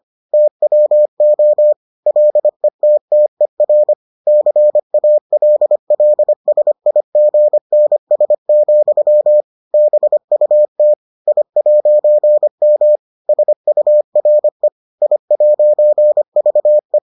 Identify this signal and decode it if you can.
Morse code, heard as a keyed tone, about 25 wpm, 600 Hz. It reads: TWO LETTER CALLSIGNS, BUT I'M SURE I'VE